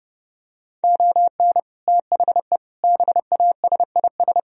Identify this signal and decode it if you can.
Morse code, heard as a keyed tone, about 30 wpm, 700 Hz. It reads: ON THE BASIS